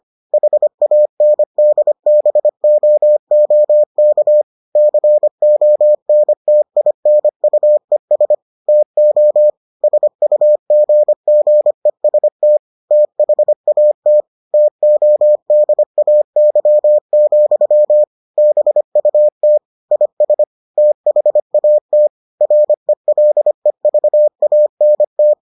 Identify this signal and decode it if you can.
Morse code, heard as a keyed tone, about 25 wpm, 600 Hz. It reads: HANDBOOK CONTINUES TO SUGGEST THAT TODAY, BUT IS THAT RELEVANT